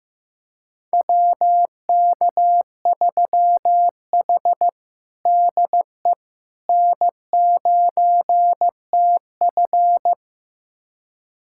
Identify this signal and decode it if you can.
Morse code, heard as a keyed tone, about 16 wpm, 700 Hz. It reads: WK3H DE N9TF
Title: USA Callsign DE Callsign - 4 Characters - 15wpm